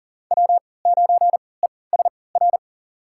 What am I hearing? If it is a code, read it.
Morse code, heard as a keyed tone, about 40 wpm, spaced as for 23 wpm, 700 Hz. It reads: W9ESR
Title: USA Call Signs - 3x Character Spacing - 40wpm